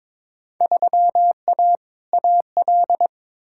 Morse code, heard as a keyed tone, about 22 wpm, 700 Hz. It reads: 3A AL
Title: Field Day Exchanges - 22wpm